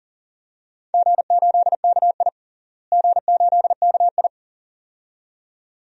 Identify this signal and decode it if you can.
Morse code, heard as a keyed tone, about 40 wpm, 700 Hz. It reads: G8KI G8KI
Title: DXCC-100 Call Signs - 4 Characters - Repeated 2x - 40wpm